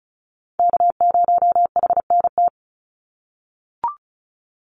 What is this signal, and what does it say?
Morse code, heard as a keyed tone, about 34 wpm, 700 Hz. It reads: K0HNT
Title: USA Call Signs 35wpm